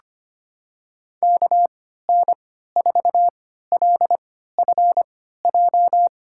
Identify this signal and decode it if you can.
Morse code, heard as a keyed tone, about 25 wpm, spaced as for 14 wpm, 700 Hz. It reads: KN4LFJ
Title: USA Call Signs - 6 Characters - 3x Character Spacing - 25wpm